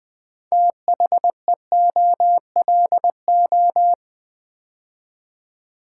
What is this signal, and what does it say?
Morse code, heard as a keyed tone, about 20 wpm, 700 Hz. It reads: THEOLO
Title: Build A Word - 3 Letters At A Time - 8-Letter Words - 20wpm